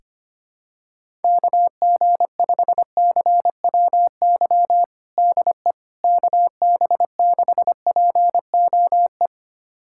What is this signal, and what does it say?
Morse code, heard as a keyed tone, about 25 wpm, 700 Hz. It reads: KG5CWY DE KB6POE
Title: USA Callsign DE Callsign - 6 Characters - 25wpm